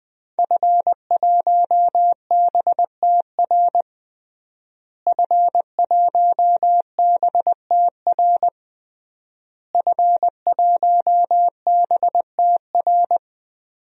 Morse code, heard as a keyed tone, about 21 wpm, 700 Hz. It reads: F1BTR F1BTR F1BTR
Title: DXCC-100 Call Signs - 5 Characters - Repeated 3x - 20wpm